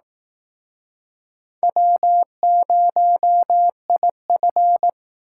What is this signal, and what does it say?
Morse code, heard as a keyed tone, about 18 wpm, 700 Hz. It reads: W0IF
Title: Speed Racing - USA Call Signs 15wpm